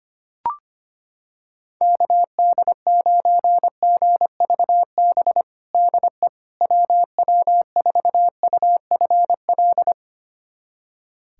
Morse code, heard as a keyed tone, about 25 wpm, 700 Hz. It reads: KD9GVB DE WW4UFL
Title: USA Callsign DE Callsign - 6 Characters - 25wpm